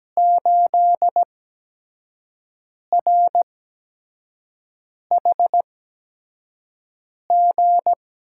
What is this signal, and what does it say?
Morse code, heard as a keyed tone, about 17 wpm, spaced as for 5 wpm, 700 Hz. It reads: 8RHG
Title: USA Call Signs - 8x Character Spacing - 17wpm